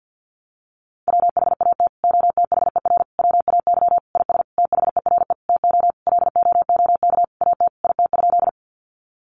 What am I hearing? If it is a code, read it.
Morse code, heard as a keyed tone, about 50 wpm, 700 Hz. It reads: WHAT OTHER WAY IS THERE TO LOOK AT IT?
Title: Sentences from Top 100 Words 50wpm